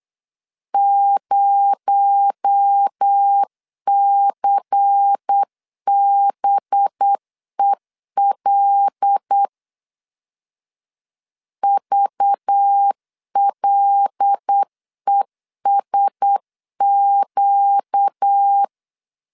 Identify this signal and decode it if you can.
Morse code, heard as a keyed tone, about 8 wpm, 800 Hz. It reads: 0CBEL VLESQ
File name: 8wpm